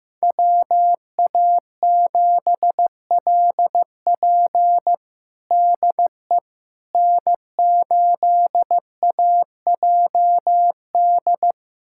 Morse code, heard as a keyed tone, about 16 wpm, 700 Hz. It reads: WA7LP DE N8AJD
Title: USA Callsign DE Callsign - 5 Characters - 15wpm